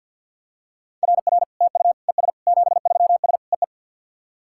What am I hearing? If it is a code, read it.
Morse code, heard as a keyed tone, about 53 wpm, 700 Hz. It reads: RR TU ES 73S EE
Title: SOTA QSO Practice 50wpm